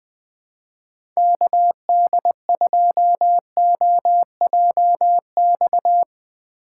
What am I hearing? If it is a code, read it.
Morse code, heard as a keyed tone, about 20 wpm, 700 Hz. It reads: KD2OJX
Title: USA Call Signs - 6 Characters - 4x Character Spacing - 20wpm